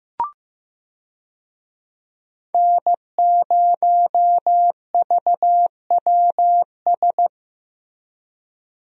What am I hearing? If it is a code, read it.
Morse code, heard as a keyed tone, about 15 wpm, 700 Hz. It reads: N0VWS
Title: USA Call Signs - 5 Characters - Repeated 3x - 15wpm